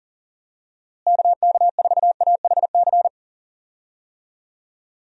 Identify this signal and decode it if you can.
Morse code, heard as a keyed tone, about 40 wpm, 700 Hz. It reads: KK4AHC
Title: USA Call Signs - 6 Characters - Repeated 3x - 40wpm